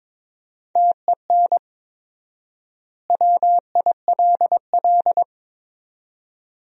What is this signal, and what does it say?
Morse code, heard as a keyed tone, about 21 wpm, 700 Hz. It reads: TEN WILL
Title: 4x Word-Spacing - Sentences from Top 400 Words - 22wpm